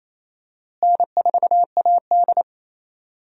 Morse code, heard as a keyed tone, about 28 wpm, 700 Hz. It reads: N4AD
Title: USA Call Signs - 4 Characters - Repeated 3x - 28wpm